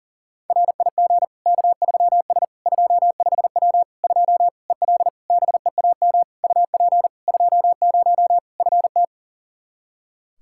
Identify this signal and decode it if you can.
Morse code, heard as a keyed tone, about 37 wpm, 700 Hz. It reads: RIG K3S 25W 2 EL BEAM UP 20 FT
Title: Practice QSO - Exchange 2 - Info - 40wpm